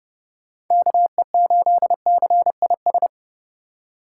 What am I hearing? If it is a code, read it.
Morse code, heard as a keyed tone, about 30 wpm, 700 Hz. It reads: KE8CIS
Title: USA Call Signs - 6 Characters - 4x Character Spacing - 30wpm